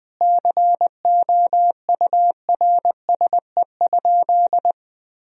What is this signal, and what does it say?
Morse code, heard as a keyed tone, about 19 wpm, 700 Hz. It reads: COURSE?
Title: Sentences from Top 1000 Words - Review Part 2 - 20wpm